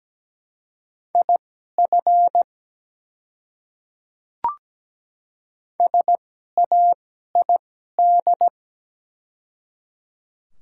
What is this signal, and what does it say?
Morse code, heard as a keyed tone, about 17 wpm, spaced as for 12 wpm, 690 Hz. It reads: IF SAID
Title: 2x Character Spacing - Top 100 Words - 17wpm